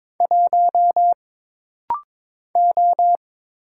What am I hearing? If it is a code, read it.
Morse code, heard as a keyed tone, about 22 wpm, 700 Hz. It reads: 1 O
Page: This rapid-fire timing is geared to push your ICR (Instant Character Recognition) skills.